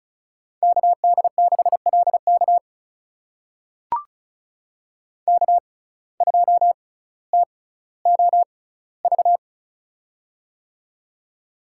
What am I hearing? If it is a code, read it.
Morse code, heard as a keyed tone, about 36 wpm, spaced as for 18 wpm, 700 Hz. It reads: KD6LK K2TOV
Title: USA Call Signs - 5 Characters - 6x Character Spacing - 35wpm